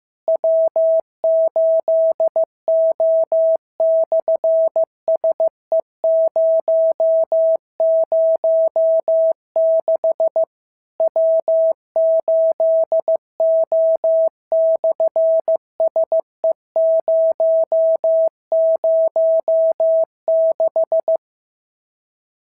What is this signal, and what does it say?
Morse code, heard as a keyed tone, about 15 wpm, 650 Hz. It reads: W8O/SE006 W8O/SE006
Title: SOTA QSO Practice 15wpm